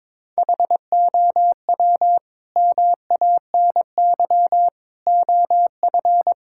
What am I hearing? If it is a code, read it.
Morse code, heard as a keyed tone, about 22 wpm, 700 Hz. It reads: HOW MANY OF
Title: Sentences from Top 1000 Words - Review Part 6 - 22wpm